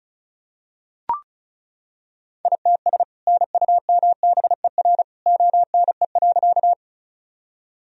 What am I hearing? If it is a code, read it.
Morse code, heard as a keyed tone, about 36 wpm, 700 Hz. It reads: ITS NUMBER ONE.
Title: Sentences from Top 1000 Words - Review Part 1 - 35wpm